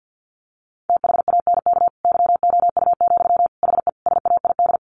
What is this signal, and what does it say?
Morse code, heard as a keyed tone, about 50 wpm, 700 Hz. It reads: THANK YOU, HE SAID
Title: Sentences from Top 1500 Words 50wpm